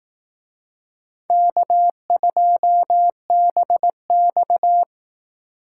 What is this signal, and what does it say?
Morse code, heard as a keyed tone, about 18 wpm, 700 Hz. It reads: K2BX
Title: Speed Racing - USA 1x2 Call Signs 15wpm